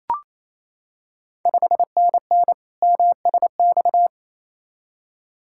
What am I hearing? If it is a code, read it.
Morse code, heard as a keyed tone, about 27 wpm, 690 Hz. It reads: 5NN MSX
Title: Ontario QSO Party Exchange 28wpm